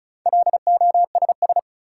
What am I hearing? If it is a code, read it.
Morse code, heard as a keyed tone, about 36 wpm, 700 Hz. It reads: LOSS
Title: Sentences from Top 1400 Words - New Word Drill Alt - 35wpm